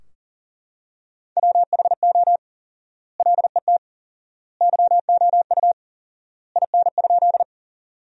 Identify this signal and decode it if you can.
Morse code, heard as a keyed tone, about 40 wpm, 700 Hz. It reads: WHO LET YOU IN?
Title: 4x Word-Spacing - Sentences from Top 300 Words - 40wpm